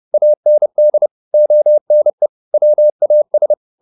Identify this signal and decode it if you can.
Morse code, heard as a keyed tone, about 30 wpm, 600 Hz. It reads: AND ONE WAS